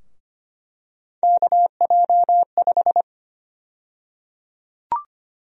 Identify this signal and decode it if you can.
Morse code, heard as a keyed tone, about 25 wpm, 700 Hz. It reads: KJ5
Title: USA Call Sign Prefixes - 25wpm